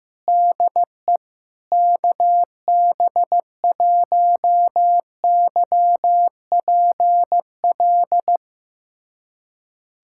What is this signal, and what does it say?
Morse code, heard as a keyed tone, about 16 wpm, 700 Hz. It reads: DE KB1YPL
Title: USA Callsign DE Callsign - 6 Characters - 15wpm